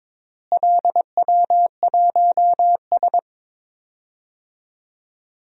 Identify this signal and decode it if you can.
Morse code, heard as a keyed tone, about 22 wpm, 700 Hz. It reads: LW1S
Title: DXCC-100 Call Signs - 4 Characters - Repeated 2x - 22wpm